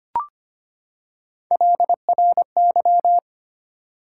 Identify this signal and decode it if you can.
Morse code, heard as a keyed tone, about 25 wpm, 700 Hz. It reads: LRY
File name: Three Letter-Number Combinations - 25wpm